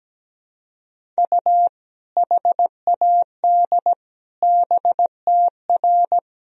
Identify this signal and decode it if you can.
Morse code, heard as a keyed tone, about 17 wpm, 700 Hz. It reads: U HAD BTR
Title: Sentences from Top 400 Words and Abbreviations 17wpm